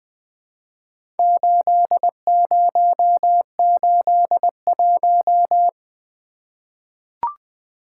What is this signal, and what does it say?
Morse code, heard as a keyed tone, about 20 wpm, 700 Hz. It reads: 8081
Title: Four Letter-Number Combinations - 20wpm